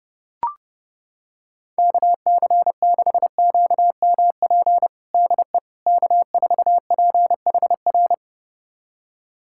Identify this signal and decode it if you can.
Morse code, heard as a keyed tone, about 31 wpm, 700 Hz. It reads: KC6QMP DE K4PHR
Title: USA Callsign DE Callsign 30wpm